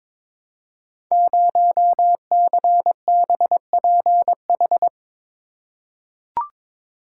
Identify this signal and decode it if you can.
Morse code, heard as a keyed tone, about 22 wpm, 700 Hz. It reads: 0CBPH
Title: Code Groups - 5 Letter:Numbers - 22wpm